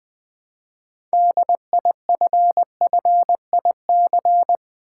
Morse code, heard as a keyed tone, about 20 wpm, 700 Hz. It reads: DIFFIC
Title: Build A Word - 3 Letters At A Time - 9-Letter Words - 20wpm